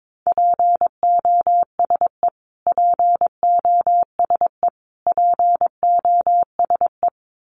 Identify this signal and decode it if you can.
Morse code, heard as a keyed tone, about 22 wpm, 700 Hz. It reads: POSE POSE POSE
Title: Top 500 Words Repeated 22wpm